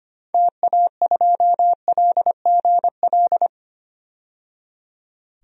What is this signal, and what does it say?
Morse code, heard as a keyed tone, about 25 wpm, 700 Hz. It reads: TA2LGL
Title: DXCC-100 Call Signs - 6 Characters - 25wpm